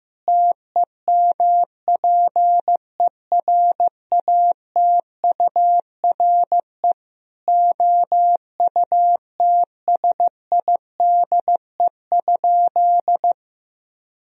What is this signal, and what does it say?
Morse code, heard as a keyed tone, about 15 wpm, 700 Hz. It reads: TEMPERATURE OUTSIDE?
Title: Sentences from Top 1000 Words - Review Part 4 - 15wpm